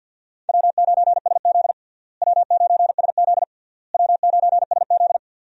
Morse code, heard as a keyed tone, about 52 wpm, 700 Hz. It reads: W9SZ W9SZ W9SZ
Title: USA Call Signs - 4 Characters - Repeated 3x - 50wpm